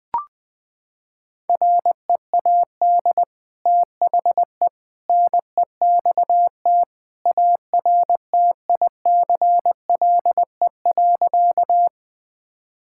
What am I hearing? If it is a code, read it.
Morse code, heard as a keyed tone, about 19 wpm, 700 Hz. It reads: READ THE NEXT ARTICLE.
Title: Sentences from Top 1000 Words - Review Part 5 - 20wpm